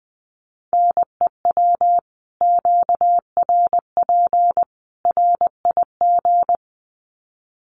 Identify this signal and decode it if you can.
Morse code, heard as a keyed tone, about 20 wpm, 700 Hz. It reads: NEW QRP RIG
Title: Common QSO Phrases 20wpm